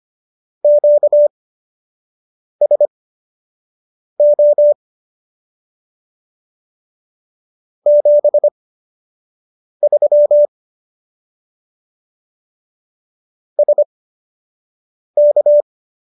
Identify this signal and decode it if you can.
Morse code, heard as a keyed tone, about 25 wpm, spaced as for 6 wpm, 590 Hz. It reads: QSO 73 SK
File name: qso002 06